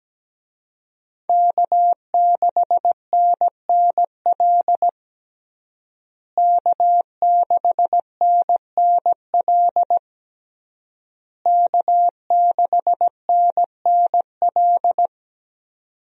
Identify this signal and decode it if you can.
Morse code, heard as a keyed tone, about 17 wpm, 700 Hz. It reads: K6NNL K6NNL K6NNL
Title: USA Call Signs - 5 Characters - Repeated 3x - 17wpm